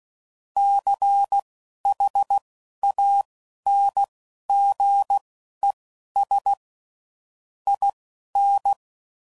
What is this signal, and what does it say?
Morse code, heard as a keyed tone, about 16 wpm, spaced as for 11 wpm, 800 Hz. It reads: CHANGES IN
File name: CW Code Groups @10WPM / Technical Subjects / Reactance